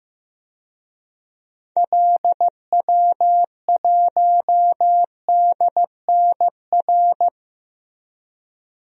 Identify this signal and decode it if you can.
Morse code, heard as a keyed tone, about 15 wpm, 700 Hz. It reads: LW1DNR
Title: DXCC-100 Call Signs - 6 Characters - Repeated 2x - 15wpm